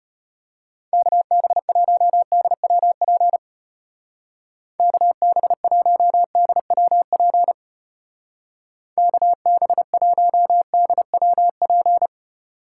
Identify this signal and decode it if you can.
Morse code, heard as a keyed tone, about 31 wpm, 700 Hz. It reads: KB1DWP KB1DWP KB1DWP
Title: Speed Racing - USA Call Signs 25wpm